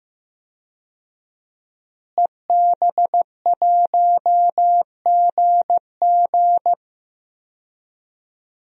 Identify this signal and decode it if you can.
Morse code, heard as a keyed tone, about 15 wpm, 700 Hz. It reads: EB1GG
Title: DXCC-100 Call Signs - 5 Characters - Repeated 2x - 15wpm